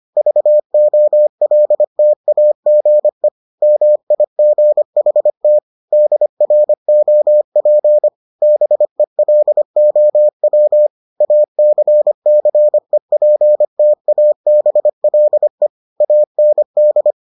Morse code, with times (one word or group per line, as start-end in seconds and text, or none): 0.2-3.3 VOLTAGE
3.6-5.6 MIGHT
5.9-8.1 DROP
8.4-10.9 BELOW
11.2-15.7 ACCEPTABLE
16.0-17.1 AND